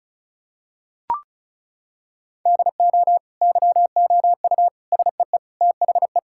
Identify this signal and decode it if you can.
Morse code, heard as a keyed tone, about 36 wpm, 700 Hz. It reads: DO YOU SEE THE
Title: Sentences from Top 1600 Words 35wpm